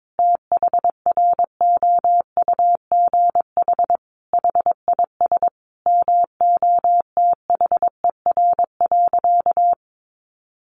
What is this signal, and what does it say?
Morse code, heard as a keyed tone, about 22 wpm, 700 Hz. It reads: THROUGH HIS MOTHER.
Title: Sentences from Top 1200 Words 22wpm